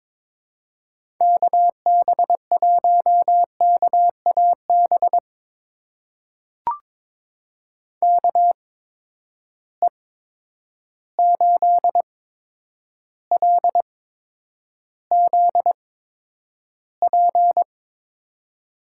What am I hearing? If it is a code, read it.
Morse code, heard as a keyed tone, about 21 wpm, spaced as for 9 wpm, 700 Hz. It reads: KB1KAB KE8LZP
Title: USA Call Signs - 6 Characters - 8x Character Spacing - 22wpm